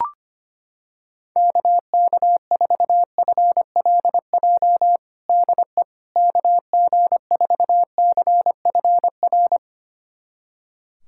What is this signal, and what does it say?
Morse code, heard as a keyed tone, about 25 wpm, 700 Hz. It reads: KK4FLJ DE KG4CFR
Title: USA Callsign DE Callsign - 6 Characters - 25wpm